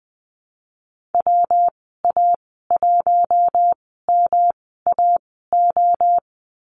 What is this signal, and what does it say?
Morse code, heard as a keyed tone, about 20 wpm, spaced as for 15 wpm, 700 Hz. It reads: WA1MAO
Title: 161 Lesson 26g - Letter g - Call Signs 20wpm